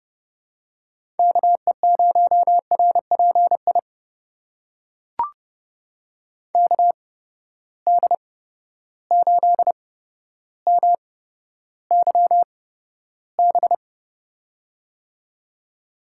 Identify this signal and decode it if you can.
Morse code, heard as a keyed tone, about 30 wpm, spaced as for 13 wpm, 700 Hz. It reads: KE0RPI KD8MYB
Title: USA Call Signs - 8x Character Spacing - 30wpm